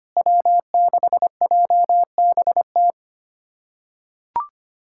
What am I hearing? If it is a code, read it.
Morse code, heard as a keyed tone, about 25 wpm, 700 Hz. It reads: W6JBT
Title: USA Callsign DE Callsign - 5 Characters - 25wpm